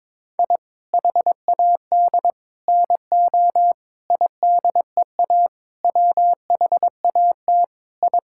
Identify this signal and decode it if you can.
Morse code, heard as a keyed tone, about 22 wpm, 700 Hz. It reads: I HAD NO IDEA WHAT I
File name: practice / 22wpm / Sets of 7 Words - 22wpm